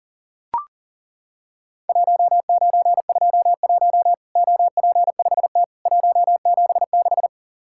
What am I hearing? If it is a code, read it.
Morse code, heard as a keyed tone, about 40 wpm, 700 Hz. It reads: 1921 OP5T 176
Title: Worked All Europe DX Contest - QTC Exchange - 40wpm